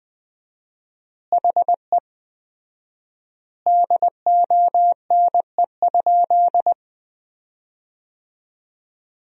Morse code, heard as a keyed tone, about 19 wpm, 700 Hz. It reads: HE DONE?
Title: 4x Word-Spacing - Sentences from Top 500 Words - 20wpm